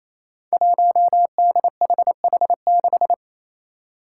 Morse code, heard as a keyed tone, about 28 wpm, 700 Hz. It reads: 1DHH6
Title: Five Challenging Letter-Number Combinations 28wpm